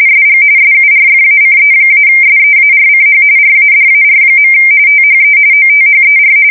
RTTY, about 45.45 baud. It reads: SG SURFACE-SEARCH RADAR. THE OTHER
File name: fsk170 / USS-Missouri